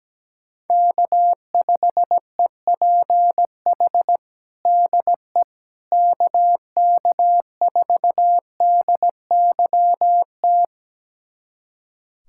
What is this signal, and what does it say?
Morse code, heard as a keyed tone, about 17 wpm, 700 Hz. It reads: K5EPH DE KK4DYT
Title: USA Callsign DE Callsign 17wpm